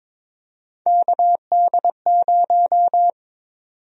Morse code, heard as a keyed tone, about 22 wpm, 700 Hz. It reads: KD0
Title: USA Call Sign Prefixes - Repeated 2x - 22wpm